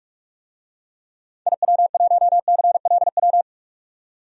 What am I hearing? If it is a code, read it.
Morse code, heard as a keyed tone, about 46 wpm, 700 Hz. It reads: IW1CLW
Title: DXCC-100 Call Signs - 6 Characters - 45wpm